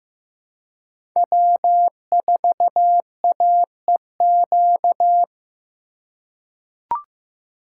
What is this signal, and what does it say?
Morse code, heard as a keyed tone, about 15 wpm, 700 Hz. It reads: W4AEQ
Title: USA Call Signs - 5 Characters - 2x Character Spacing - 15wpm